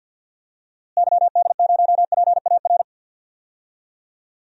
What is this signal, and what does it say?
Morse code, heard as a keyed tone, about 48 wpm, 700 Hz. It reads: YD0PAR
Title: DXCC-100 Call Signs - 6 Characters - Repeated 2x - 50wpm